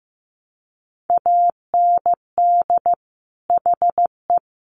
Morse code, heard as a keyed tone, about 15 wpm, 700 Hz. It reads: AND HE
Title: Sets of 3 Words 15wpm